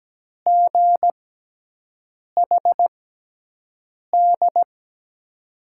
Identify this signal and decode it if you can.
Morse code, heard as a keyed tone, about 17 wpm, spaced as for 6 wpm, 700 Hz. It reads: GHD
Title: USA Call Signs - 5 Characters - 6x Character Spacing - 17wpm